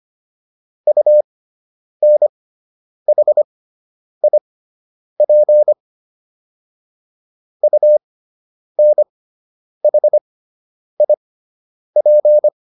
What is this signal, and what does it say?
Morse code, heard as a keyed tone, about 25 wpm, spaced as for 9 wpm, 600 Hz. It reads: UNHIP UNHIP